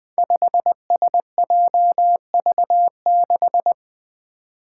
Morse code, heard as a keyed tone, about 21 wpm, 700 Hz. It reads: 5SJV6
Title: Five Challenging Letter-Number Combinations 20wpm